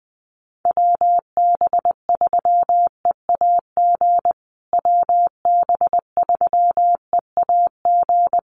Morse code, heard as a keyed tone, about 20 wpm, 700 Hz. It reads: WB3EAG WB3EAG
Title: USA Call Signs Repeated 3x 20wpm